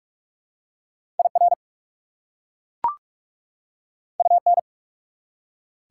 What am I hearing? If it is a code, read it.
Morse code, heard as a keyed tone, about 44 wpm, 690 Hz. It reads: IR UN
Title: Prefixes and Suffixes 45wpm